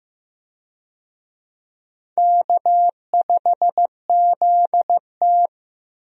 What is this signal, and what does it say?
Morse code, heard as a keyed tone, about 15 wpm, 700 Hz. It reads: K5ZT
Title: USA Call Signs - 4 Characters - Repeated 2x - 15wpm